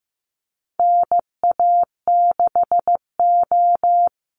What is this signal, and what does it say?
Morse code, heard as a keyed tone, about 15 wpm, 700 Hz. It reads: NA6O
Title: NA SOTA Chaser Callsigns 15wpm